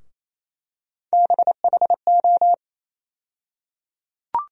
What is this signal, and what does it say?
Morse code, heard as a keyed tone, about 28 wpm, 700 Hz. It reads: BHO
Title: USA Call Sign Postfixes - 3 Letters 28wpm